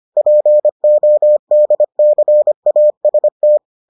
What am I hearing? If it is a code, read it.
Morse code, heard as a keyed tone, about 25 wpm, 600 Hz. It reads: PODCAST